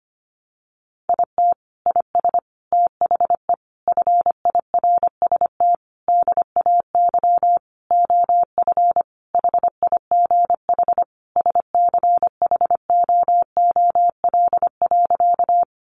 Morse code, heard as a keyed tone, about 25 wpm, 700 Hz. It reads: IT IS THE FIRST DAY OF HIGH SCHOOL.
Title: Additional Easy Phrases 25wpm